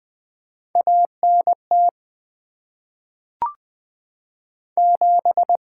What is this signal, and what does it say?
Morse code, heard as a keyed tone, about 20 wpm, 700 Hz. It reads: ANT 7
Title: Top 20 QSO Elements - 6x Character Spacing - 20wpm